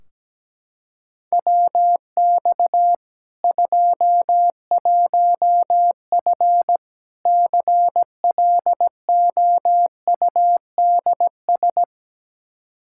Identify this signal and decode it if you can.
Morse code, heard as a keyed tone, about 17 wpm, 700 Hz. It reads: WX 21F CLOUDS
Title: K1USN SST QSO Practice 17wpm